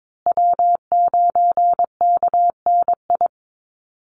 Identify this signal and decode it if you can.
Morse code, heard as a keyed tone, about 22 wpm, 700 Hz. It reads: W9KNI
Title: Call Signs Repeated 3x - 22wpm